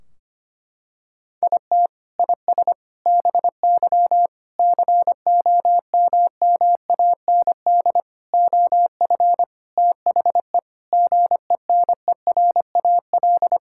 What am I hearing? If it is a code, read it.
Morse code, heard as a keyed tone, about 25 wpm, 700 Hz. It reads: IT IS BY COMMAND OF THE GENERAL
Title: Sentences from Top 1600 Words 25wpm